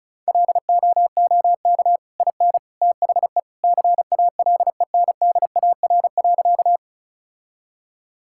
Morse code, heard as a keyed tone, about 36 wpm, 700 Hz. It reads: LOOK IN THE CALENDAR.
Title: Sentences from Top 1600 Words 35wpm